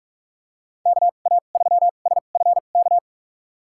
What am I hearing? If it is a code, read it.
Morse code, heard as a keyed tone, about 47 wpm, spaced as for 33 wpm, 700 Hz. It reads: KA3SFK
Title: USA Call Signs - 6 Characters - 2x Character Spacing - 45wpm